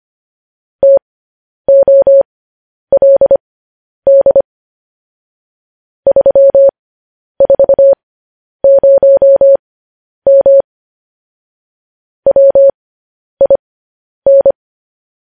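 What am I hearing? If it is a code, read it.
Morse code, heard as a keyed tone, about 25 wpm, spaced as for 10 wpm, 550 Hz. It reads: TOLD 340M WIN
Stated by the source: Morse code news headlines sent at 10.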